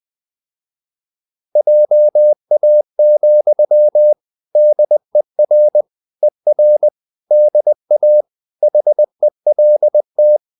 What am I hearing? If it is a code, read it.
Morse code, heard as a keyed tone, about 20 wpm, 600 Hz. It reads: JA, DER ER DA HELT
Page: CW med Gnister